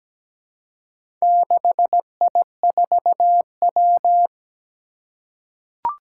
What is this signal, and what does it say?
Morse code, heard as a keyed tone, about 17 wpm, 700 Hz. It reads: 6I4W
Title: Four Challenging Letter-Number Combinations 17wpm